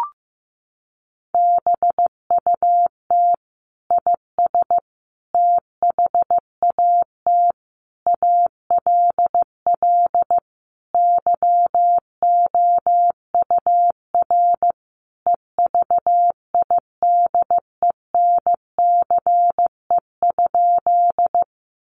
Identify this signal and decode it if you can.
Morse code, heard as a keyed tone, about 15 wpm, 700 Hz. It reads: BUT IS THAT ALL YOUR EVIDENCE?
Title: Sentences from Top 1300 Words 15wpm